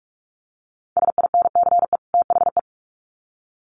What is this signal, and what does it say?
Morse code, heard as a keyed tone, about 44 wpm, 700 Hz. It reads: SINCE THE
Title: Sets of 2 Words 45wpm